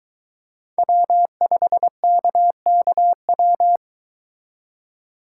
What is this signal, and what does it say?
Morse code, heard as a keyed tone, about 22 wpm, 700 Hz. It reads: W5KKW